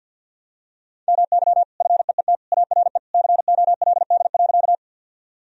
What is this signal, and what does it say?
Morse code, heard as a keyed tone, about 52 wpm, 700 Hz. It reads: MY FEET ARE COLD.
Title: Sentences from Top 1000 Words - Review Part 1 - 50wpm